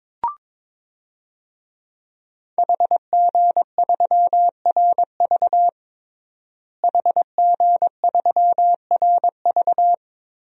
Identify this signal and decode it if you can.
Morse code, heard as a keyed tone, about 22 wpm, 700 Hz. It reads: HG3RV HG3RV
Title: DXCC-100 Call Signs - 5 Characters - Repeated 2x - 22wpm